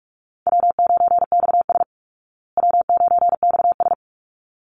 Morse code, heard as a keyed tone, about 44 wpm, 700 Hz. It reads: W9XS W9XS
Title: Call Signs Repeated 3x - 45wpm